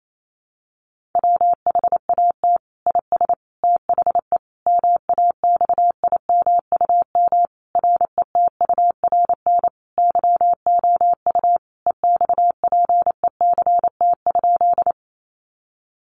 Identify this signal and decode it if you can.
Morse code, heard as a keyed tone, about 28 wpm, 700 Hz. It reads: WHAT IS THE MAXIMUM RETURN YOU EXPECT?
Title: Sentences from Top 1300 Words 28wpm